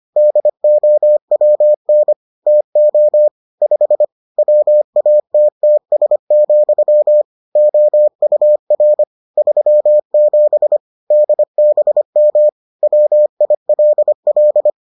Morse code, heard as a keyed tone, about 25 wpm, 600 Hz. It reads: DOWN TO 5 WATTS, OUR 37 DBM WILL